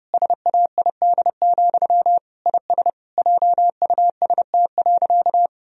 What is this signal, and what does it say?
Morse code, heard as a keyed tone, about 32 wpm, 700 Hz. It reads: SAID, IS JUST.
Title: Sentences from Top 1000 Words - Review Part 4 - 30wpm